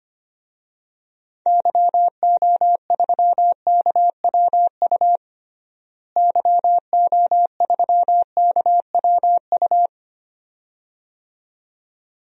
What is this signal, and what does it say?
Morse code, heard as a keyed tone, about 25 wpm, 700 Hz. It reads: YO3KWU YO3KWU
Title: DXCC-100 Call Signs - 6 Characters - Repeated 2x - 25wpm